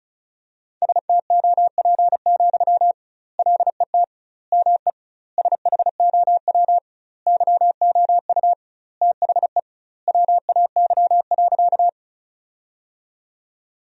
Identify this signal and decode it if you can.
Morse code, heard as a keyed tone, about 34 wpm, 700 Hz. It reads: STOP, LET ME SHOW YOU THE WAY.
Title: 2x Word-Spacing - Sentences from Top 300 Words - 35wpm